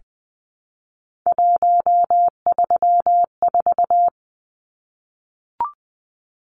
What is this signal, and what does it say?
Morse code, heard as a keyed tone, about 20 wpm, 700 Hz. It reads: 134
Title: Numbers - 3-Digit 20wpm